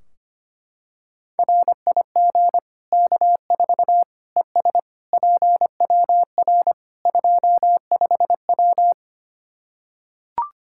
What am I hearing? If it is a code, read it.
Morse code, heard as a keyed tone, about 25 wpm, 700 Hz. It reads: RIG K4 ES PWR 25W
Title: Practice QSO - Exchange 2 - Info - 25wpm